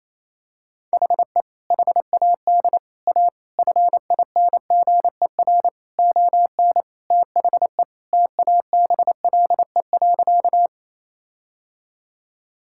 Morse code, heard as a keyed tone, about 28 wpm, 700 Hz. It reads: HE HAD A FINGER ON THE TABLE.
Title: Sentences from Top 1600 Words 28wpm